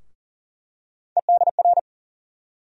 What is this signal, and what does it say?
Morse code, heard as a keyed tone, about 40 wpm, 700 Hz. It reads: EDR
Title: USA Call Sign Postfixes - 3 Letters 40wpm